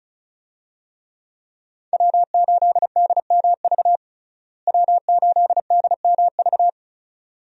Morse code, heard as a keyed tone, about 36 wpm, 700 Hz. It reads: W8DMV W8DMV
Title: USA Call Signs - 5 Characters - Repeated 2x - 35wpm